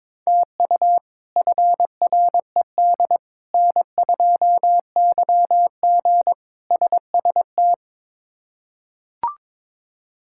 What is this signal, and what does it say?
Morse code, heard as a keyed tone, about 22 wpm, 700 Hz. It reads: TU FRED N2YG SST
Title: K1USN SST QSO Practice - 22wpm